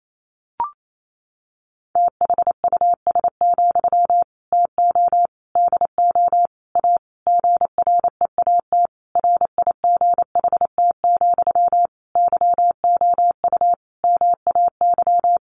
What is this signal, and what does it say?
Morse code, heard as a keyed tone, about 28 wpm, 690 Hz. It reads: THUS, TO DO A GREAT RIGHT, YOU MAY
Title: Sentences from Top 900 Words 28wpm